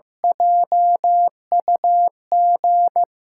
Morse code, heard as a keyed tone, about 15 wpm, 700 Hz. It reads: JUG
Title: USA Call Signs - 6 Characters - 8x Character Spacing - 15wpm